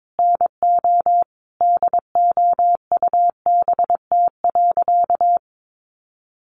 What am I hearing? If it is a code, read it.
Morse code, heard as a keyed tone, about 22 wpm, 700 Hz. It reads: NO DOUBT.
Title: Sentences from Top 1500 Words 22wpm